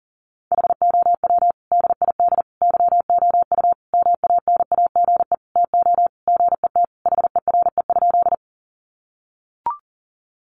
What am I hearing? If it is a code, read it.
Morse code, heard as a keyed tone, about 40 wpm, 700 Hz. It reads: HOW DID YOU MANAGE TO GET HERE?
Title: Sentences from Top 1400 Words 40wpm